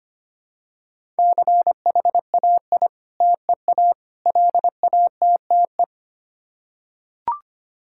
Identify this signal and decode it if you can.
Morse code, heard as a keyed tone, about 25 wpm, 700 Hz. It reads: CHAI TEA LATTE
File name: practice / 25wpm / Difficult Words with Hints - 25wpm